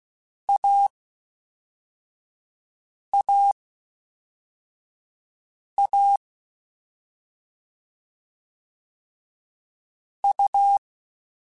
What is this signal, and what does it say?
Morse code, heard as a keyed tone, about 16 wpm, 800 Hz. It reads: A A A U
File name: Difficult Combos / Difficult Combos-mp3s / AUV AUV4 @5